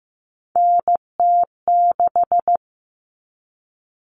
Morse code, heard as a keyed tone, about 15 wpm, 700 Hz. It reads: NT6
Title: All Cut Numbers - 3-Digit 15wpm